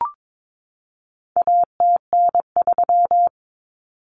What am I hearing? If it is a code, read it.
Morse code, heard as a keyed tone, about 22 wpm, 700 Hz. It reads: ATN3
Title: Common Cut Numbers - 4-Digit 22wpm